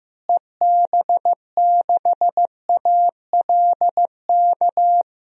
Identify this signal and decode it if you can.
Morse code, heard as a keyed tone, about 15 wpm, 700 Hz. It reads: EB6ALK
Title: DXCC-100 Call Signs - 6 Characters - Repeated 3x - 15wpm